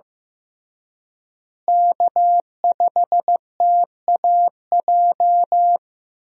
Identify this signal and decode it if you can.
Morse code, heard as a keyed tone, about 15 wpm, 700 Hz. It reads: K5TAJ
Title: USA Call Signs - 5 Characters - Repeated 3x - 15wpm